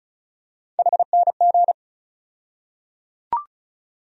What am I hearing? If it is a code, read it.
Morse code, heard as a keyed tone, about 34 wpm, 700 Hz. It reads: HNG
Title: USA Call Sign Postfixes - 3 Letters 35wpm